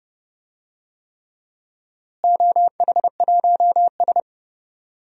Morse code, heard as a keyed tone, about 30 wpm, 700 Hz. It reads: OH1S
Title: DXCC-100 Call Signs - 4 Characters - Repeated 3x - 30wpm